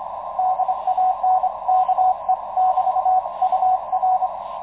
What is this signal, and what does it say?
Morse code, heard as a keyed tone, about 25 wpm, 740 Hz, with radio noise. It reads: DANKE <BT> UR
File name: Expedition 2019 / ua1ceg-p 10.22.2019 09.36 utc audio DL1HSI